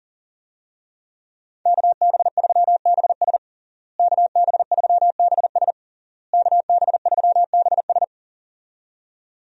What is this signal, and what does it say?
Morse code, heard as a keyed tone, about 40 wpm, 700 Hz. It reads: KB3BS KB3BS KB3BS
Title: USA Call Signs - 5 Characters - Repeated 3x - 40wpm